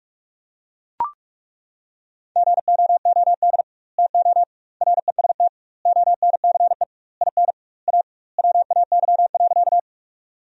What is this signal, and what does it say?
Morse code, heard as a keyed tone, about 42 wpm, 700 Hz. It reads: GOOD TO REST ONCE IN A WAY.
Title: 2x Word-Spacing - Sentences from Top 500 Words - 45wpm